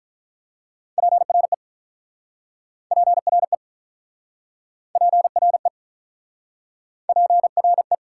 Morse code, heard as a keyed tone, about 43 wpm, 700 Hz. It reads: PRE PRE PRE PRE
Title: Speed-Racing Prefixes and Suffixes 35wpm